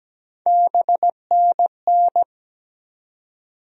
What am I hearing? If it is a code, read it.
Morse code, heard as a keyed tone, about 17 wpm, 700 Hz. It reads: BNN
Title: USA Call Sign Postfixes - 3 Letters 17wpm